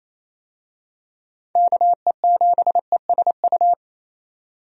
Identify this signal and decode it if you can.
Morse code, heard as a keyed tone, about 28 wpm, 700 Hz. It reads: KE7ESU
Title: USA Call Signs - 6 Characters - 28wpm